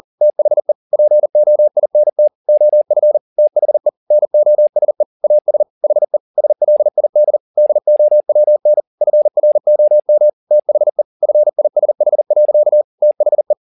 Morse code, heard as a keyed tone, about 40 wpm, 600 Hz. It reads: THE POINT OF THE NOSE AS HE SLID DOWN FROM THE FISH. THE
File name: the old man and the sea 2 40wpm